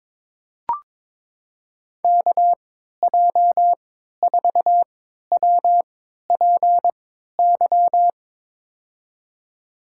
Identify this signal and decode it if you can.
Morse code, heard as a keyed tone, about 22 wpm, spaced as for 12 wpm, 700 Hz. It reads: KJ4WPY
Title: USA Call Signs - 6 Characters - 3x Character Spacing - 22wpm